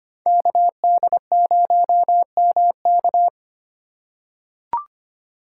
Morse code, heard as a keyed tone, about 25 wpm, 700 Hz. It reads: KD0MK
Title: USA Call Signs - 5 Characters - 4x Character Spacing - 25wpm